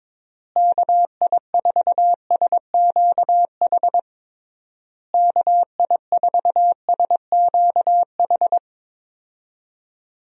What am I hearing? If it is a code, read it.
Morse code, heard as a keyed tone, about 22 wpm, 700 Hz. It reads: KI4SQH KI4SQH
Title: USA Call Signs - 6 Characters - Repeated 3x - 22wpm